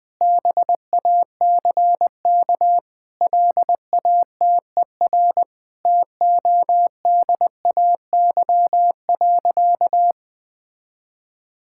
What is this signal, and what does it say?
Morse code, heard as a keyed tone, about 20 wpm, 700 Hz. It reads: BACK LATER TODAY.
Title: Sentences with 4-Word Phrases - 20wpm